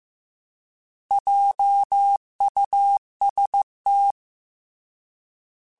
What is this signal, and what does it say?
Morse code, heard as a keyed tone, about 15 wpm, 800 Hz. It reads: JUST
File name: Alphabetical / Alphabetical mp3s / Js @13